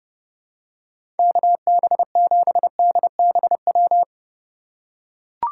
Morse code, heard as a keyed tone, about 30 wpm, 700 Hz. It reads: KB7DBW
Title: USA Call Signs - 6 Characters - 8x Character Spacing - 30wpm